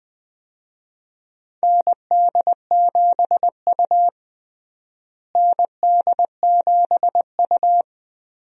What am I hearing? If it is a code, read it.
Morse code, heard as a keyed tone, about 19 wpm, 700 Hz. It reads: ND7U ND7U
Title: USA Call Signs - 4 Characters - Repeated 3x - 20wpm